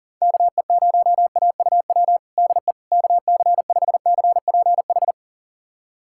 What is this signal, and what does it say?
Morse code, heard as a keyed tone, about 40 wpm, 700 Hz. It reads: KE0AUW DE KC5CPH
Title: USA Callsign DE Callsign - 6 Characters - 40wpm